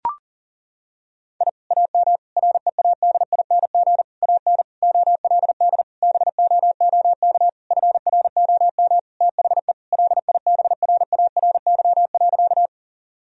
Morse code, heard as a keyed tone, about 37 wpm, 700 Hz. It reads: I AM READING AN OLD BOOK FROM THE LIBRARY.
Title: Sentences from Top 1000 Words - Review Part 4 - 40wpm